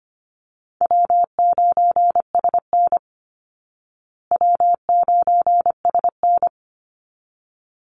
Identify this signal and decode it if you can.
Morse code, heard as a keyed tone, about 25 wpm, 700 Hz. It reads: W9SN W9SN
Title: Call Signs Repeated 3x - 25wpm